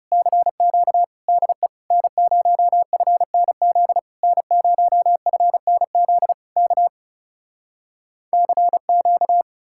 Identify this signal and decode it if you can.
Morse code, heard as a keyed tone, about 40 wpm, 700 Hz. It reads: CQ DE N0FNZ N0FNZ K CQ